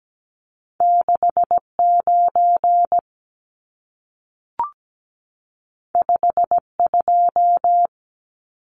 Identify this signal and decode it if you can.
Morse code, heard as a keyed tone, about 17 wpm, 700 Hz. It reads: 69 52
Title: Numbers - 2-Digit 17wpm